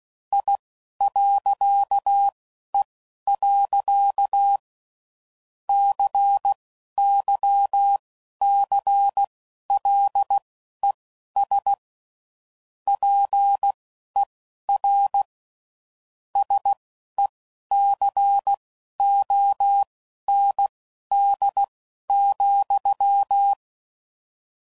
Morse code, heard as a keyed tone, about 16 wpm, spaced as for 11 wpm, 800 Hz. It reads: I.E. CYCLES PER SECOND,
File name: CW Code Groups @10WPM / Technical Subjects / Reactance